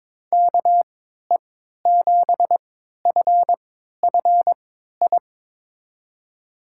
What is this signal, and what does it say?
Morse code, heard as a keyed tone, about 22 wpm, spaced as for 13 wpm, 700 Hz. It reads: KE7FFI
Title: USA Call Signs - 6 Characters - 3x Character Spacing - 22wpm